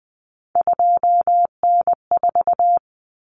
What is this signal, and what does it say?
Morse code, heard as a keyed tone, about 20 wpm, 700 Hz. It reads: 2N4
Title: Common Cut Numbers - 3-Digit 20wpm